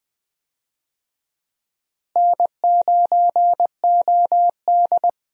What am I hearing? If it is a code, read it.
Morse code, heard as a keyed tone, about 20 wpm, 700 Hz. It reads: N9OD
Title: USA Call Signs - 4 Characters - Repeated 2x - 20wpm